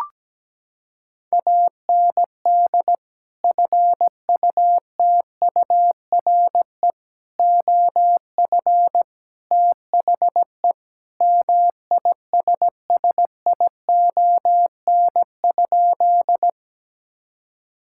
Morse code, heard as a keyed tone, about 17 wpm, 700 Hz. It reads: AND FUTURE OF THE MISSION?
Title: Sentences from Top 1600 Words 17wpm